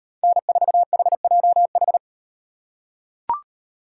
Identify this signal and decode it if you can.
Morse code, heard as a keyed tone, about 36 wpm, 700 Hz. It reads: N4HJH